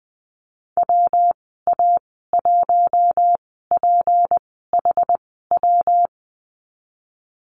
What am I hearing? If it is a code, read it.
Morse code, heard as a keyed tone, about 20 wpm, spaced as for 15 wpm, 700 Hz. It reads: WA1PHW
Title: 161 Lesson 26g - Letter g - Call Signs 20wpm